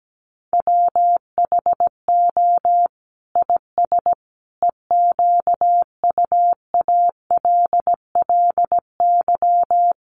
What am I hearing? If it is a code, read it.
Morse code, heard as a keyed tone, about 17 wpm, 700 Hz. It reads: WHO IS EQUALLY
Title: Sentences from Top 1400 Words 17wpm